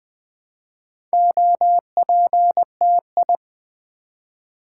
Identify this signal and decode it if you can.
Morse code, heard as a keyed tone, about 20 wpm, 700 Hz. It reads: OPTI
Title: Build A Word - 2 Letters At A Time - 8-Letter Words - 20wpm